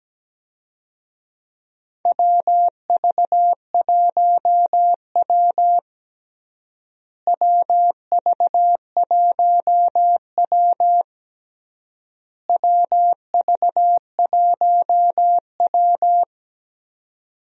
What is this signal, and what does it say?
Morse code, heard as a keyed tone, about 17 wpm, 700 Hz. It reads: WV1W WV1W WV1W
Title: USA Call Signs - 4 Characters - Repeated 3x - 17wpm